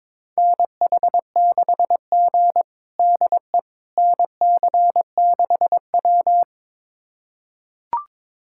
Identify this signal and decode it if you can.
Morse code, heard as a keyed tone, about 22 wpm, 700 Hz. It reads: NH6G DE NC6W
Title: USA Callsign DE Callsign - 4 Characters - 22wpm